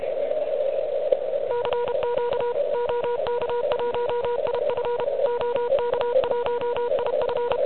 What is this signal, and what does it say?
Morse code, heard as a keyed tone, about 32 wpm, 970 Hz, with noise. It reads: CQ OK1IF OK1IF
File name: ARCHIV / QCX / QCX Pileup